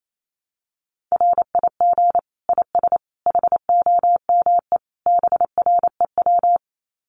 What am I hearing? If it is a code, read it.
Morse code, heard as a keyed tone, about 28 wpm, 700 Hz. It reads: RIG IS HOME BREW
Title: Common QSO Phrases 28wpm